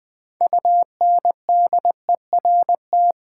Morse code, heard as a keyed tone, about 20 wpm, 700 Hz. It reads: UNDERT
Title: Build A Word - 3 Letters At A Time - 9-Letter Words - 20wpm